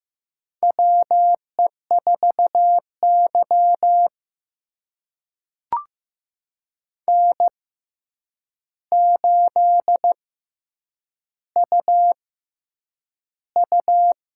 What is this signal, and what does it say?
Morse code, heard as a keyed tone, about 15 wpm, spaced as for 8 wpm, 700 Hz. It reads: WE4Y N8UU
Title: USA Call Signs - 4 Characters - 6x Character Spacing - 15wpm